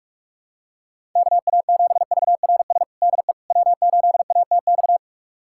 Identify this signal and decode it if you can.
Morse code, heard as a keyed tone, about 45 wpm, 700 Hz. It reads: KA7VRS DE W8ATX
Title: USA Callsign DE Callsign 45wpm